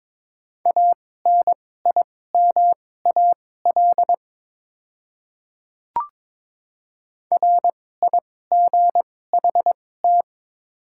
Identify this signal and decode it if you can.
Morse code, heard as a keyed tone, about 22 wpm, spaced as for 16 wpm, 700 Hz. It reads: ANIMAL RIGHT
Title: 2x Character Spacing - Top 200 Words - 22wpm